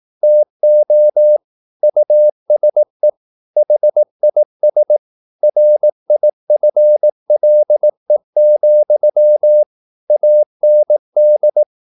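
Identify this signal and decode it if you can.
Morse code, heard as a keyed tone, about 18 wpm, 600 Hz. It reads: TO USE HIS RIFLE, AND
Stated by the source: Morse practice at 18 wpm, English text, from CW med Gnister.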